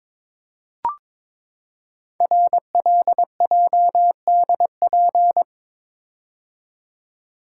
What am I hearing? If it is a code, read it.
Morse code, heard as a keyed tone, about 22 wpm, 700 Hz. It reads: RLJDP
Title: Code Groups - 5 Letters - 22wpm